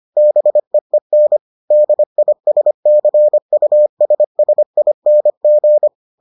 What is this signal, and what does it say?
Morse code, heard as a keyed tone, about 25 wpm, 600 Hz. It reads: BEEN DISCUSSING